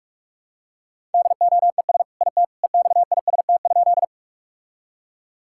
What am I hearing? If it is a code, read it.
Morse code, heard as a keyed tone, about 46 wpm, 700 Hz. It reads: DOES IT EXIST?
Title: Sentences from Top 1000 Words - Review Part 5 - 45wpm